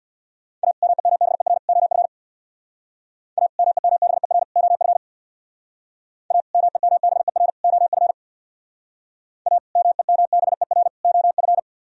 Morse code, heard as a keyed tone, about 60 wpm, 700 Hz. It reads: A MEMBER OF A MEMBER OF A MEMBER OF A MEMBER OF
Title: Speed-Racing - Sets of 3 Words 50wpm